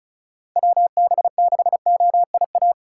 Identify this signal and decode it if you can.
Morse code, heard as a keyed tone, about 36 wpm, 700 Hz. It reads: WB6OIA
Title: USA Call Signs - 6 Characters - 3x Character Spacing - 35wpm